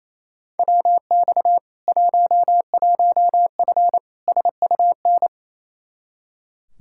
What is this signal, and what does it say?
Morse code, heard as a keyed tone, about 28 wpm, 700 Hz. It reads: WX 11F SUN
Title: Practice QSO - Exchange 2 - Info - 28wpm